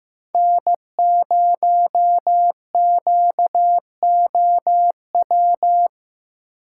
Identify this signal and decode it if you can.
Morse code, heard as a keyed tone, about 15 wpm, 700 Hz. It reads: N0QOW
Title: USA Call Signs - 5 Characters - 6x Character Spacing - 15wpm